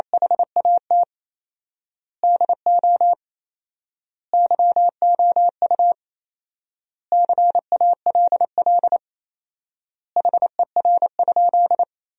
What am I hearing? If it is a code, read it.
Morse code, heard as a keyed tone, about 28 wpm, 700 Hz. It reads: HAT DO YOU CALL HER?
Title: 4x Word-Spacing - Sentences from Top 100 Words - 28wpm